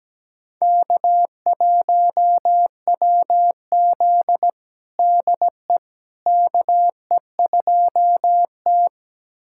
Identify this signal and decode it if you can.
Morse code, heard as a keyed tone, about 17 wpm, 700 Hz. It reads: K1WZ DE KE2T
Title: USA Callsign DE Callsign - 4 Characters - 17wpm